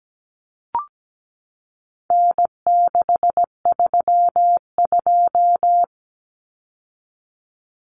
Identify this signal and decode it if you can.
Morse code, heard as a keyed tone, about 17 wpm, 700 Hz. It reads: N632
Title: Common Cut Numbers - 4-Digit 17wpm